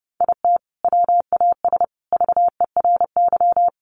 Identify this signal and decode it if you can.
Morse code, heard as a keyed tone, about 30 wpm, 700 Hz. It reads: IT WAS VERY
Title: Sentences from Top 1100 Words 30wpm